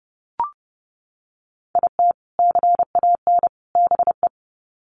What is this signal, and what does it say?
Morse code, heard as a keyed tone, about 30 wpm, 700 Hz. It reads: IT CAN BE
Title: Sets of 3 Words 30wpm